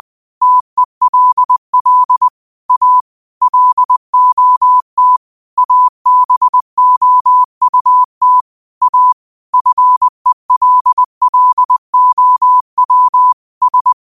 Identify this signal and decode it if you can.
Morse code, heard as a keyed tone, about 20 wpm, 1000 Hz. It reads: TELL A LOT ABOUT A FELLOW S
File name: qotdfiles / 20wpm